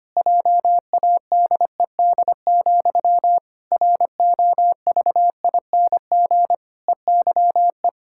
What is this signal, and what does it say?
Morse code, heard as a keyed tone, about 25 wpm, 700 Hz. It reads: JADED, ROVING EYE
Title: Pangram Challenge v2 - 25wpm